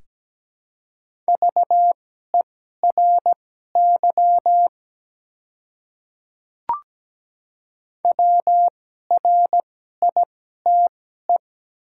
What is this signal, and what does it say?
Morse code, heard as a keyed tone, about 17 wpm, spaced as for 12 wpm, 700 Hz. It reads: VERY WRITE
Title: 2x Character Spacing - Top 200 Words - 17wpm